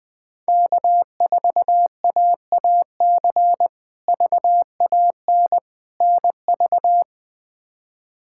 Morse code, heard as a keyed tone, about 20 wpm, 700 Hz. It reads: K4AAC VAN NV
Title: K1USN SST QSO Practice - 20wpm